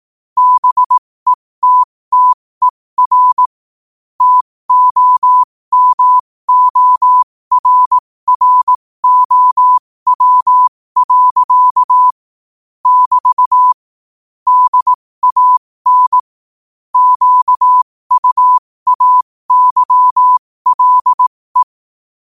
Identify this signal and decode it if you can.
Morse code, heard as a keyed tone, about 18 wpm, spaced as for 15 wpm, 1000 Hz. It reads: BETTER TOMORROW. <BT> DAN QUAYLE
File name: qotdfiles / 15wpm